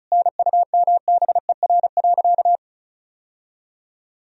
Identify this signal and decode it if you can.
Morse code, heard as a keyed tone, about 36 wpm, 700 Hz. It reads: NUMBER.
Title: Sentences from Top 1000 Words - Review Part 1 - 35wpm